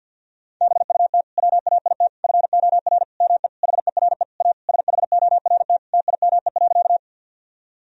Morse code, heard as a keyed tone, about 50 wpm, 700 Hz. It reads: BUT WAIT FOR ME HERE A SHORT TIME.
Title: Sentences from Top 1000 Words - Review Part 2 - 50wpm